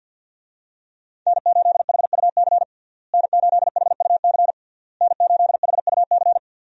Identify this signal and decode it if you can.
Morse code, heard as a keyed tone, about 53 wpm, 700 Hz. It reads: N8HUC N8HUC N8HUC
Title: USA Call Signs - 5 Characters - Repeated 3x - 50wpm